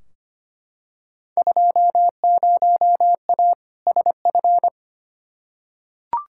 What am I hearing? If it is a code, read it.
Morse code, heard as a keyed tone, about 25 wpm, 700 Hz. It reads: 20A SF
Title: Field Day Exchanges Encore - 25wpm